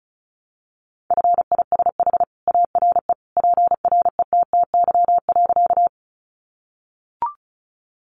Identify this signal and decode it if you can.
Morse code, heard as a keyed tone, about 34 wpm, 700 Hz. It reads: FISH ARE PRETTY.
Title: Easy Phrases 35wpm